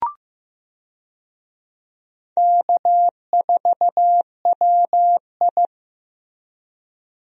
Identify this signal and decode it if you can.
Morse code, heard as a keyed tone, about 15 wpm, 700 Hz. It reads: K4WI
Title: USA Call Signs - 4 Characters - Repeated 2x - 15wpm